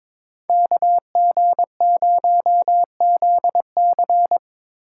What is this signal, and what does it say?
Morse code, heard as a keyed tone, about 22 wpm, 700 Hz. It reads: KG0ZC
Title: USA Call Signs - 5 Characters - 3x Character Spacing - 22wpm